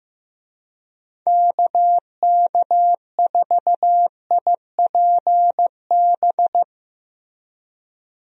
Morse code, heard as a keyed tone, about 15 wpm, 700 Hz. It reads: KK4IPB
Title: USA Call Signs - 6 Characters - Repeated 2x - 15wpm